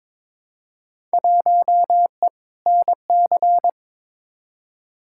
Morse code, heard as a keyed tone, about 21 wpm, 700 Hz. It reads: 1E NC
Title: Field Day Exchanges - 22wpm